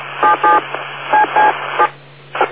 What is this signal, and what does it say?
DTMF, heard as telephone touch tones, tones of about 170 ms. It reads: 77 CC7